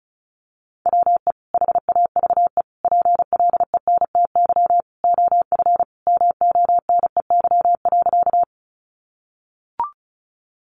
Morse code, heard as a keyed tone, about 34 wpm, 700 Hz. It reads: WE HAVE PLENTY OF MONEY.
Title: Sentences from Top 1300 Words 35wpm